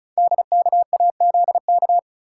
Morse code, heard as a keyed tone, about 36 wpm, 690 Hz. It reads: DKAZK
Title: Code Groups - 5 Letters - 35wpm